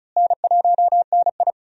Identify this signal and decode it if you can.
Morse code, heard as a keyed tone, about 36 wpm, 700 Hz. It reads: N1NI
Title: USA Call Signs - 4 Characters - Repeated 2x - 35wpm